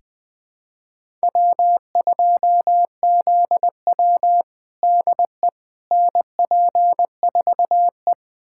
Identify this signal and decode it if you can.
Morse code, heard as a keyed tone, about 21 wpm, 700 Hz. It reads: W2ZW DE NP4E
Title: USA Callsign DE Callsign - 4 Characters - 20wpm